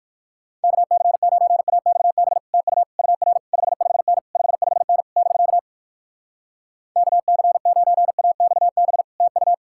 Morse code, heard as a keyed tone, about 59 wpm, 700 Hz. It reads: KC9AXB TU UR 55N 55N <BK> KC9AXB TU